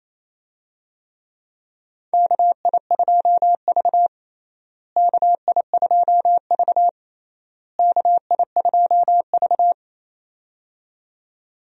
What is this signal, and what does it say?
Morse code, heard as a keyed tone, about 27 wpm, 700 Hz. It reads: KI2V KI2V KI2V
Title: USA Call Signs - 4 Characters - Repeated 3x - 28wpm